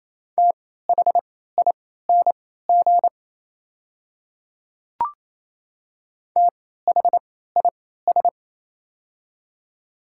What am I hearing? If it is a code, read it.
Morse code, heard as a keyed tone, about 28 wpm, spaced as for 16 wpm, 690 Hz. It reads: THING THIS
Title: 3x Character Spacing - Top 100 Words - 28wpm